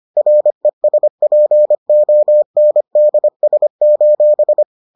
Morse code, heard as a keyed tone, about 25 wpm, 600 Hz. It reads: RESPONDS: